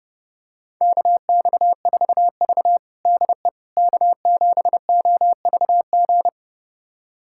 Morse code, heard as a keyed tone, about 30 wpm, 700 Hz. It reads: KX4V DE K7OVG
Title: USA Callsign DE Callsign 30wpm